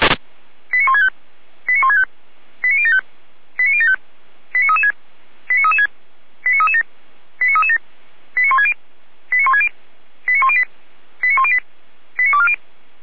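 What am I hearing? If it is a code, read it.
Five-tone selective call (ZVEI), heard as quick runs of five tones, tones of about 70 ms. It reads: 89165 89165 89075 89075 89336 89336 89337 89337 89160 89160 89108 89108 89240
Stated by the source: ZVEI five-tone selective call received on an SDR.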